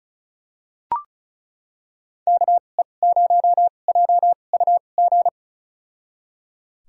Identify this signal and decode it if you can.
Morse code, heard as a keyed tone, about 36 wpm, spaced as for 25 wpm, 700 Hz. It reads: KE0JUG
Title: USA Call Signs - 6 Characters - 2x Character Spacing - 35wpm